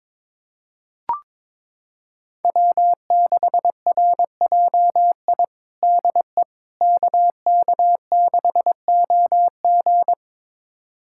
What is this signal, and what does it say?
Morse code, heard as a keyed tone, about 22 wpm, 700 Hz. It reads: W6RJI DE KK6OG
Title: USA Callsign DE Callsign - 5 Characters - 22wpm